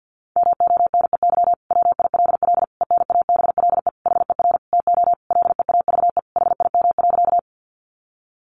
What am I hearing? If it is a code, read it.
Morse code, heard as a keyed tone, about 50 wpm, 700 Hz. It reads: MONEY WILL ENABLE HER TO LEAVE HIM.
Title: Sentences from Top 1500 Words 50wpm